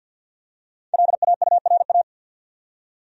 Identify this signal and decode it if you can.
Morse code, heard as a keyed tone, about 52 wpm, 700 Hz. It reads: LAURA
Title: USA Common Names - 50wpm